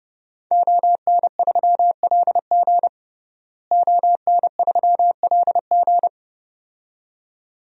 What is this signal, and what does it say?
Morse code, heard as a keyed tone, about 30 wpm, 700 Hz. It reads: ON3LG ON3LG
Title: DXCC-100 Call Signs - 5 Characters - Repeated 2x - 30wpm